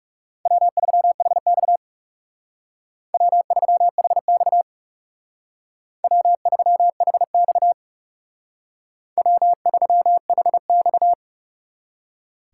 Morse code, heard as a keyed tone, about 35 wpm, 700 Hz. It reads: W3HX W3HX W3HX W3HX
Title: Speed Racing - USA 1x2 Call Signs 30wpm